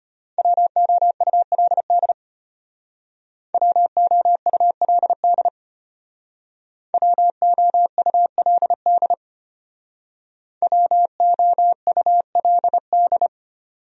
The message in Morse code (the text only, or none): WOULD WOULD WOULD WOULD